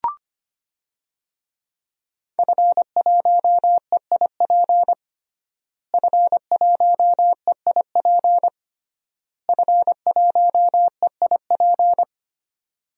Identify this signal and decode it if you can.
Morse code, heard as a keyed tone, about 25 wpm, 700 Hz. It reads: F1EIP F1EIP F1EIP
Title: DXCC-100 Call Signs - 5 Characters - Repeated 3x - 25wpm